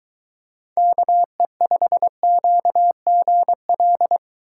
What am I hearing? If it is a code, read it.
Morse code, heard as a keyed tone, about 23 wpm, 700 Hz. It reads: KE5QGL